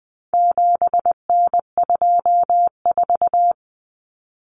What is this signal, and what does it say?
Morse code, heard as a keyed tone, about 20 wpm, 700 Hz. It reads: 7N24
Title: Common Cut Numbers - 4-Digit 20wpm